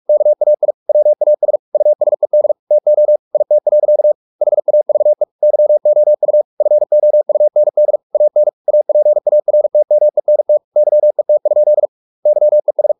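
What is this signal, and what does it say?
Morse code, heard as a keyed tone, about 44 wpm, 600 Hz. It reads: KAI WAS USED TO IT. HAVE YOU FOUND AN APARTMENT YET? YES
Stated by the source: Morse practice at 45 wpm, the novel Never by Ken Follett in English, from CW med Gnister.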